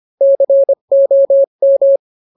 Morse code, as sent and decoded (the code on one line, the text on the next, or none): -.-. --- --
COM